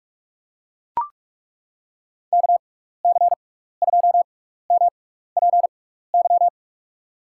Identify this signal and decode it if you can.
Morse code, heard as a keyed tone, about 41 wpm, spaced as for 15 wpm, 700 Hz. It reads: KC2MPY
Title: USA Call Signs - 6 Characters - 6x Character Spacing - 45wpm